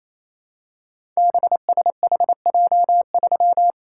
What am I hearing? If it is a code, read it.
Morse code, heard as a keyed tone, about 27 wpm, 700 Hz. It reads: BSHJ3
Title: Code Groups - 5 Letter:Numbers - 28wpm